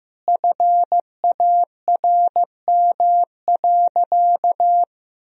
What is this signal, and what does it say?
Morse code, heard as a keyed tone, about 16 wpm, 700 Hz. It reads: FARM.
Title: Sentences from Top 1000 Words - Review Part 2 - 15wpm